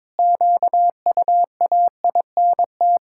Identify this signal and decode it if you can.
Morse code, heard as a keyed tone, about 22 wpm, 700 Hz. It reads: QUAINT
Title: Pangram Challenge v2 - 22wpm